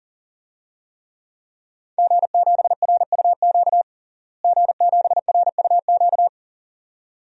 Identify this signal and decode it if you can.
Morse code, heard as a keyed tone, about 40 wpm, 700 Hz. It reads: G7RUQ G7RUQ
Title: DXCC-100 Call Signs - 5 Characters - Repeated 2x - 40wpm